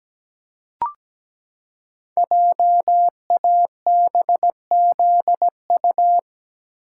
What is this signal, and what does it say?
Morse code, heard as a keyed tone, about 17 wpm, 700 Hz. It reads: JABZU
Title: Code Groups - 5 Letters - 17wpm